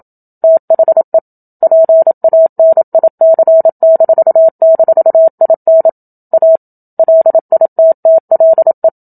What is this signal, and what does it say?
Morse code, heard as a keyed tone, about 27 wpm, 650 Hz. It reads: THE PANIC--IN A LITTLE
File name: Book Two / 01-Chapter 1 - 27 WPM